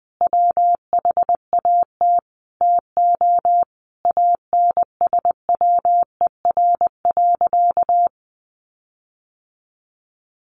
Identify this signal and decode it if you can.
Morse code, heard as a keyed tone, about 20 wpm, 700 Hz. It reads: WHAT TO ANSWER.
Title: Sentences from Top 1500 Words 20wpm